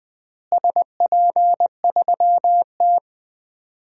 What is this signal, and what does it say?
Morse code, heard as a keyed tone, about 20 wpm, 700 Hz. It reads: SP3T
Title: DXCC-100 Call Signs - 4 Characters - Repeated 3x - 20wpm